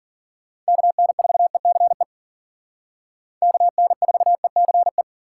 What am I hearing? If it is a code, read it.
Morse code, heard as a keyed tone, about 53 wpm, 700 Hz. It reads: KN4ECE KN4ECE